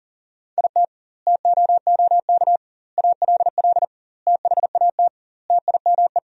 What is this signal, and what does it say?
Morse code, heard as a keyed tone, about 40 wpm, 700 Hz. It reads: IT TOOK ALL THAT TIME
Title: 2x Word-Spacing - Sentences from Top 300 Words - 40wpm